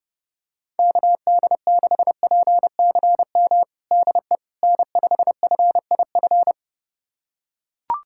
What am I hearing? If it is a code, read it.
Morse code, heard as a keyed tone, about 32 wpm, 700 Hz. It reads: KD6PCM DE N5FIF
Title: USA Callsign DE Callsign 30wpm